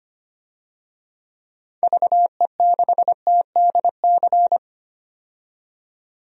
Morse code, heard as a keyed tone, about 25 wpm, 700 Hz. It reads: VE6TDC
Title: DXCC-100 Call Signs - 6 Characters - 25wpm